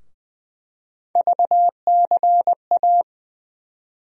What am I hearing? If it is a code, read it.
Morse code, heard as a keyed tone, about 20 wpm, 700 Hz. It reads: VCA
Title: USA Call Sign Postfixes - 3 Letters 20wpm